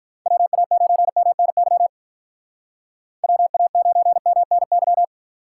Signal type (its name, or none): Morse code